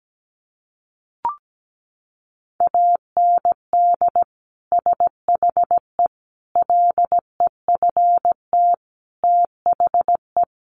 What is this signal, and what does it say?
Morse code, heard as a keyed tone, about 17 wpm, 700 Hz. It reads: AND SHE LEFT THE
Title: Sentences from Top 300 Words 17wpm